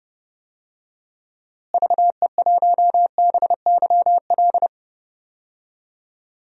Morse code, heard as a keyed tone, about 32 wpm, 700 Hz. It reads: VE1BYL
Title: DXCC-100 Call Signs - 6 Characters - Repeated 3x - 30wpm